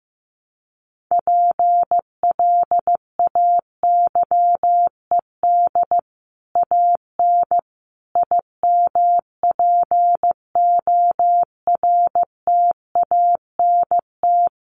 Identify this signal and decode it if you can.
Morse code, heard as a keyed tone, about 15 wpm, 700 Hz. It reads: PLAYED AN IMPORTANT
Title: Sets of 5 Words Encore2 15wpm